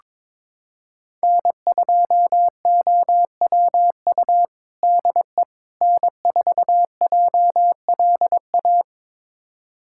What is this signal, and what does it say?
Morse code, heard as a keyed tone, about 22 wpm, 700 Hz. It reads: N2OWU DE N4JLA
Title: USA Callsign DE Callsign - 5 Characters - 22wpm